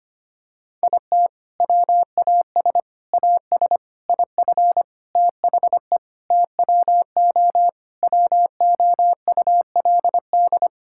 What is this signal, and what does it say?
Morse code, heard as a keyed tone, about 25 wpm, 700 Hz. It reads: IT WAS AS IF THE TWO WOULD
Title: Sentences from Top 1000 Words - Review Part 6 - 25wpm